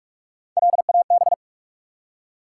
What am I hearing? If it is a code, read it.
Morse code, heard as a keyed tone, about 44 wpm, 700 Hz. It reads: LAB
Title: Three Letter Words - 45wpm